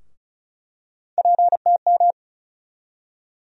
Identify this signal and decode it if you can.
Morse code, heard as a keyed tone, about 36 wpm, 700 Hz. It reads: PTM
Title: USA Call Sign Postfixes - 3 Letters 35wpm